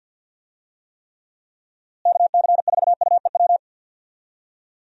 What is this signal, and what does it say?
Morse code, heard as a keyed tone, about 51 wpm, 700 Hz. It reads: KC4UEW
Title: USA Call Signs - 6 Characters - Repeated 2x - 50wpm